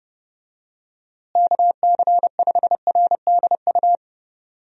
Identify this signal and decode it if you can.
Morse code, heard as a keyed tone, about 32 wpm, 700 Hz. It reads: KC5RDU
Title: USA Call Signs - 6 Characters - Repeated 3x - 30wpm